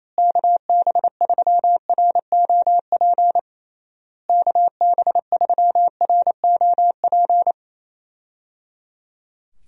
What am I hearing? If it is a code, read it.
Morse code, heard as a keyed tone, about 28 wpm, 700 Hz. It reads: KB3ROP KB3ROP
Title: USA Call Signs - 6 Characters - Repeated 3x - 28wpm